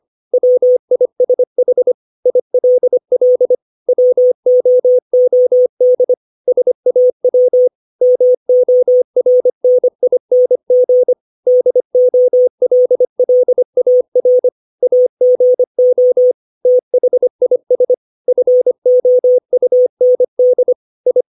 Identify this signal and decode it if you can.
Morse code, heard as a keyed tone, about 25 wpm, 500 Hz. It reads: WISH ILL WOOD SAW MORNING DOLLAR AGO THIS FOUND I
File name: 632 words / 632 Words 25WPM